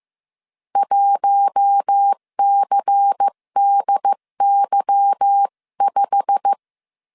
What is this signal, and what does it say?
Morse code, heard as a keyed tone, about 15 wpm, 800 Hz. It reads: 1CDY5
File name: CW / DET / 1 9 1500C S B 14wpm 3302